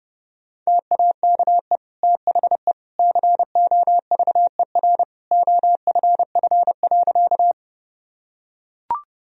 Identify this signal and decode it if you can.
Morse code, heard as a keyed tone, about 30 wpm, 700 Hz. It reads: TAKE THE COVER OFF.
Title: Sentences from Top 1000 Words - Review Part 1 - 30wpm